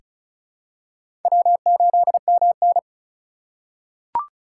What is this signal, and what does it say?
Morse code, heard as a keyed tone, about 36 wpm, 700 Hz. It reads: W8MN
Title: USA Call Signs - 4 Characters - 2x Character Spacing - 35wpm